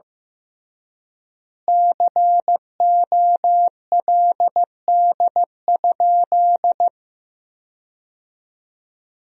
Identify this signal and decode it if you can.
Morse code, heard as a keyed tone, about 15 wpm, 700 Hz. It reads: COLD?
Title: 3x Word-Spacing - Sentences from Top 400 Words - 15wpm